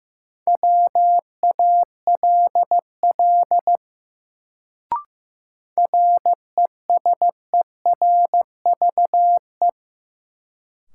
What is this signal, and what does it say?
Morse code, heard as a keyed tone, about 15 wpm, 700 Hz. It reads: WALL RESERVE
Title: Sentences from Top 1000 Words - Review Part 6 - New Word Drill - 15wpm